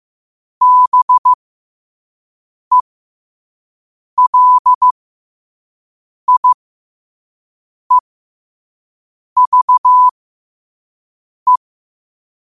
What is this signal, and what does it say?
Morse code, heard as a keyed tone, about 15 wpm, spaced as for 5 wpm, 1000 Hz. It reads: BELIEVE
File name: qotdfiles / 5wpm